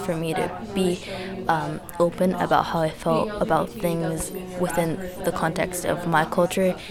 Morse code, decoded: none

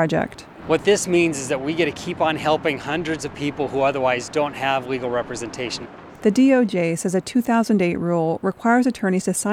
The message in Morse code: none

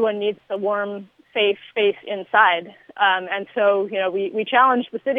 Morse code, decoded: none

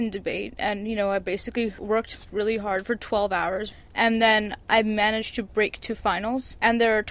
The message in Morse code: none

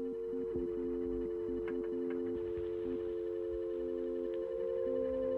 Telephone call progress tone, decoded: dial tone

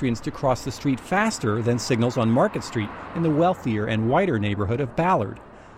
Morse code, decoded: none

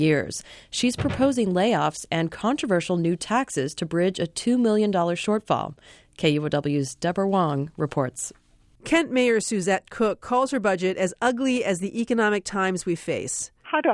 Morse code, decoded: none

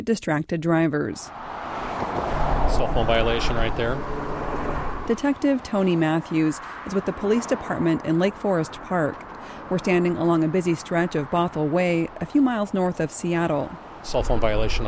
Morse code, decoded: none

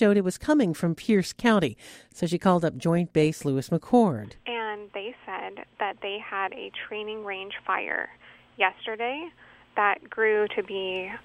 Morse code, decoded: none

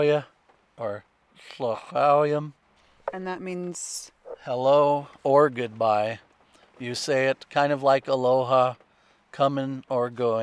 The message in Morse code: none

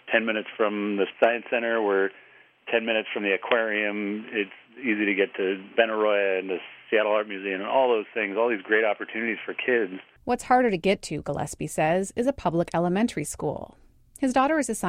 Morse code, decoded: none

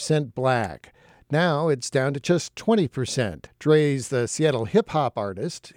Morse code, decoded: none

